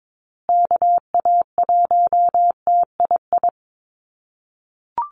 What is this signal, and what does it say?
Morse code, heard as a keyed tone, about 22 wpm, 700 Hz. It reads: KA1TII
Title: USA Call Signs Repeated 3x 22wpm